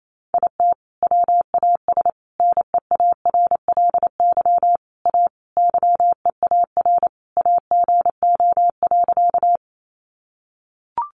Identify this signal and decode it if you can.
Morse code, heard as a keyed tone, about 28 wpm, 700 Hz. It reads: IT WAS NEARLY A YEAR AGO.
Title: Sentences from Top 1300 Words 28wpm